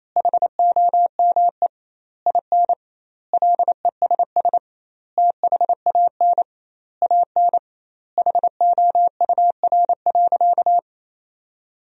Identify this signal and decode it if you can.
Morse code, heard as a keyed tone, about 28 wpm, 700 Hz. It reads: HOME IN LESS THAN AN HOUR.
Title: 2x Word-Spacing - Sentences from Top 500 Words - 28wpm